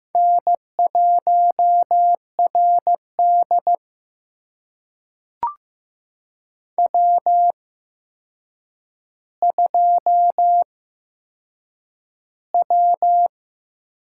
Morse code, heard as a keyed tone, about 15 wpm, spaced as for 7 wpm, 700 Hz. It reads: N1RD W2W
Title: USA Call Signs - 4 Characters - 8x Character Spacing - 15wpm